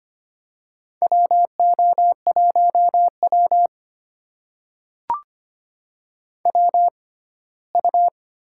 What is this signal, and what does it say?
Morse code, heard as a keyed tone, about 25 wpm, spaced as for 17 wpm, 700 Hz. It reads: WO1W WU
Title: USA Call Signs - 4 Characters - 6x Character Spacing - 25wpm